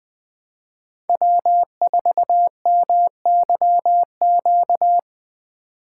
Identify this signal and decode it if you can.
Morse code, heard as a keyed tone, about 23 wpm, 700 Hz. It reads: W4MYQ
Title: Speed Racing - USA 1x3 Call Signs 20wpm